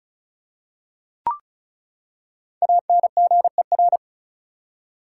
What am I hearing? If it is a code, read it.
Morse code, heard as a keyed tone, about 36 wpm, 700 Hz. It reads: ANGER
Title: Five Letter Words - 35wpm